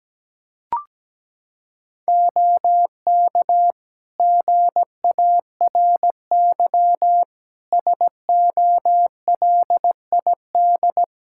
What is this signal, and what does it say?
Morse code, heard as a keyed tone, about 17 wpm, 700 Hz. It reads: OK GARY SOLID
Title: Practice QSO - Exchange 3 - More Info - 17wpm